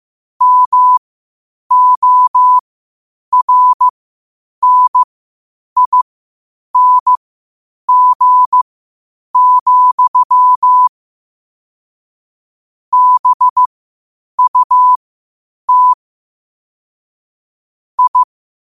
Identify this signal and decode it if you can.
Morse code, heard as a keyed tone, about 15 wpm, spaced as for 8 wpm, 1000 Hz. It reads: MORNING, BUT I